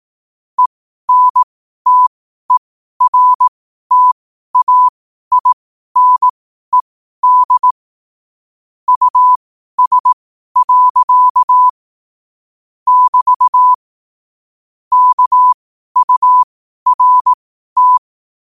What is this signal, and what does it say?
Morse code, heard as a keyed tone, about 18 wpm, spaced as for 12 wpm, 1000 Hz. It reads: ENTERTAINED US. <BT> KURT
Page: qotd-at-12-wpm